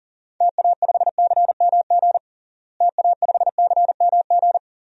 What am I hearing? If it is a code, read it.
Morse code, heard as a keyed tone, about 40 wpm, 700 Hz. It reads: TA5CMG TA5CMG
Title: DXCC-100 Call Signs - 6 Characters - Repeated 2x - 40wpm